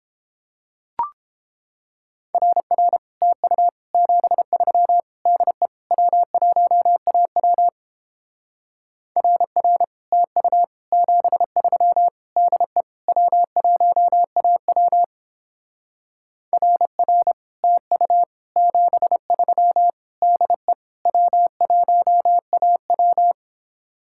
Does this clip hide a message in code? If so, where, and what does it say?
Yes: Morse code from 2.3 to 23.3 s, heard as a keyed tone, about 29 wpm, 700 Hz. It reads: RR TU 73 DE W1AW RR TU 73 DE W1AW RR TU 73 DE W1AW